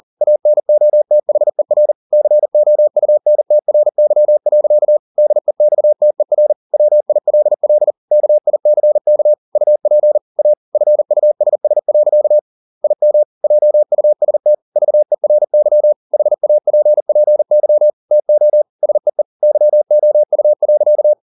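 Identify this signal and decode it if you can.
Morse code, heard as a keyed tone, about 40 wpm, 600 Hz. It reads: ANOTHER COUNTRY. DEXTER WILL KICK UP A FUSS. IM JUST VERY HAPPY TO SEE YOU.
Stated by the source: Morse code at 40 wpm, practice text, CW med Gnister.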